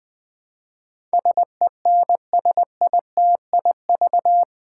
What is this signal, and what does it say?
Morse code, heard as a keyed tone, about 20 wpm, 700 Hz. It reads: SENSITIV
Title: Build A Word - 4 Letters At A Time - 11-Letter Words - 20wpm